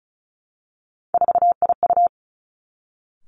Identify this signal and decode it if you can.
Morse code, heard as a keyed tone, about 33 wpm, 700 Hz. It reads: 4IU
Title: Three Challenging Letter-Number Combinations 35wpm